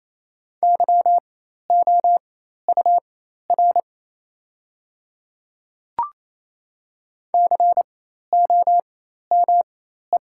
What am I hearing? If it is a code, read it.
Morse code, heard as a keyed tone, about 28 wpm, spaced as for 13 wpm, 690 Hz. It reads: YOUR COME
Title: 4x Character Spacing - Top 200 Words - 28wpm